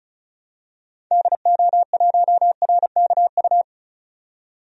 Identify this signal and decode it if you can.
Morse code, heard as a keyed tone, about 34 wpm, 700 Hz. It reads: DO1RKU
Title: DXCC-100 Call Signs - 6 Characters - 35wpm